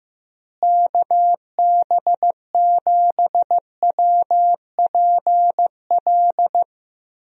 Morse code, heard as a keyed tone, about 15 wpm, 700 Hz. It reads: KB7WPL
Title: USA Call Signs - 6 Characters - Repeated 2x - 15wpm